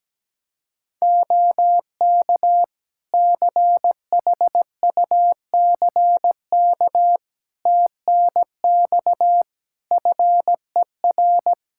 Morse code, heard as a keyed tone, about 17 wpm, 700 Hz. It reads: OK CHUCK TNX FER
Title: Practice QSO - Exchange 4 - Ending - 17wpm